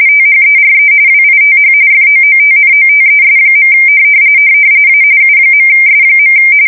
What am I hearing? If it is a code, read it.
RTTY, about 45.45 baud. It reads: THERE WITH FOOTNOTES. A U.S.
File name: fsk170 / usnavy5